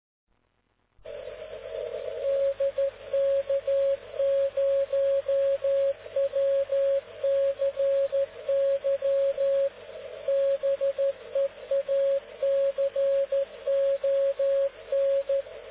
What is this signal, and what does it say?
Morse code, heard as a keyed tone, about 13 wpm, 550 Hz. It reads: DK0WCY BEACON